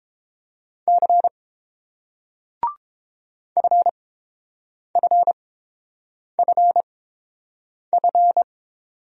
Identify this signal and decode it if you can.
Morse code, heard as a keyed tone, about 31 wpm, spaced as for 7 wpm, 700 Hz. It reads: C FFFF